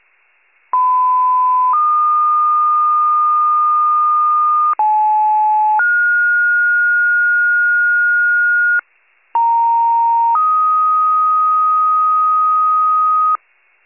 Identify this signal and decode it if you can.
Two-tone page, heard as a one-second tone, then a three-second tone, about 1010 then 1220 Hz, about 860 then 1430 Hz, about 930 then 1250 Hz. It is heard from 0.7 to 13.4 s.
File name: pager / 94 RESCUE OFC / wcalert